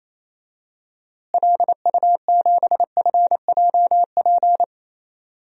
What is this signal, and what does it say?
Morse code, heard as a keyed tone, about 28 wpm, 700 Hz. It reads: LU7FJP
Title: DXCC-100 Call Signs - 6 Characters - Repeated 3x - 28wpm